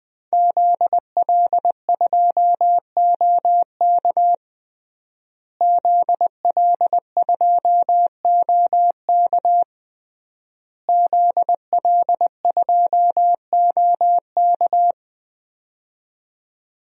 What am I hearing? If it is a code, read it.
Morse code, heard as a keyed tone, about 20 wpm, 700 Hz. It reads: ZL2OK ZL2OK ZL2OK
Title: DXCC-100 Call Signs - 5 Characters - Repeated 3x - 20wpm